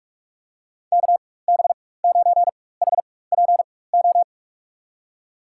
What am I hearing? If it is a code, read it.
Morse code, heard as a keyed tone, about 46 wpm, spaced as for 21 wpm, 700 Hz. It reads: KB9HPO
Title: USA Call Signs - 6 Characters - 4x Character Spacing - 45wpm